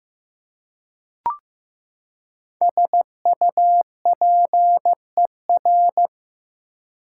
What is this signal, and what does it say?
Morse code, heard as a keyed tone, about 15 wpm, 700 Hz. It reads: SUPER
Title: Prefixes and Suffixes 15wpm